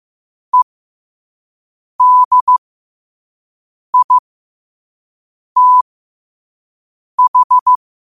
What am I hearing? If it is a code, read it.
Morse code, heard as a keyed tone, about 15 wpm, spaced as for 5 wpm, 1000 Hz. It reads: EDITH
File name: qotdfiles / 5wpm